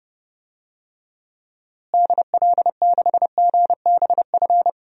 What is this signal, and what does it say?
Morse code, heard as a keyed tone, about 29 wpm, 700 Hz. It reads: DL6GBF
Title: DXCC-100 Call Signs - 6 Characters - Repeated 2x - 30wpm